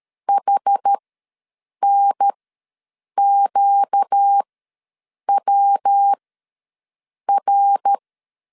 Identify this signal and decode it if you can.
Morse code, heard as a keyed tone, about 13 wpm, spaced as for 7 wpm, 800 Hz. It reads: HNQWR
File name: LEZIONI / 8 YJ45 / 8 Y ws 12wpm 3409